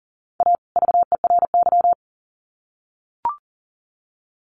Morse code, heard as a keyed tone, about 40 wpm, 700 Hz. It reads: A VERY
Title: Sets of 2 Words 40wpm